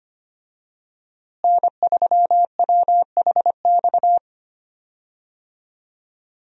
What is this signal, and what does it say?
Morse code, heard as a keyed tone, about 25 wpm, 700 Hz. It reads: N3WHX
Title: USA Call Signs - 5 Characters - Repeated 3x - 25wpm